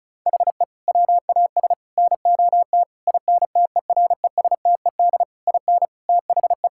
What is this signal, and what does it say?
Morse code, heard as a keyed tone, about 36 wpm, 700 Hz. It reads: HE WAS NOT INTERESTED IN THE
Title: Sentences from Top 1000 Words - Review Part 6 - 35wpm